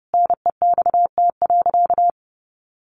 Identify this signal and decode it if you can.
Morse code, heard as a keyed tone, about 30 wpm, 700 Hz. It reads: NEXT.
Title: Sentences from Top 1200 Words 30wpm